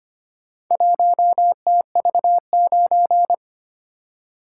Morse code, heard as a keyed tone, about 25 wpm, 700 Hz. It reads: 1TV9
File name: practice / 25wpm / Four Letter-Number Combinations - 25wpm